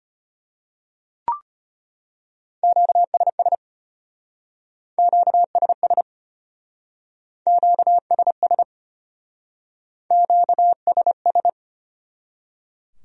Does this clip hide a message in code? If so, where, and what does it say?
Yes: Morse code from 2.6 to 11.5 s, heard as a keyed tone, about 33 wpm, 700 Hz. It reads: QSS QSS QSS QSS